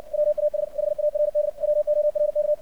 Morse code, heard as a keyed tone, about 62 wpm, 600 Hz. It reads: WAS HARD WORK
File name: archived / CRSnet